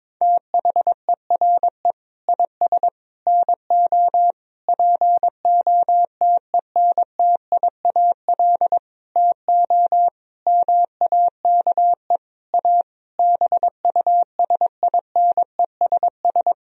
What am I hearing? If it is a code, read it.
Morse code, heard as a keyed tone, about 22 wpm, 700 Hz. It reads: THERE IS NO POTENTIAL TO MAKE A BUSINESS
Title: Sentences from Top 1000 Words - Review Part 4 - 22wpm